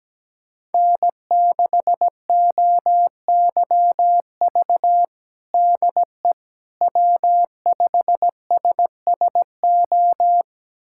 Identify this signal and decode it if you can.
Morse code, heard as a keyed tone, about 17 wpm, 700 Hz. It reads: N6OYV DE W5SSO
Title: USA Callsign DE Callsign - 5 Characters - 17wpm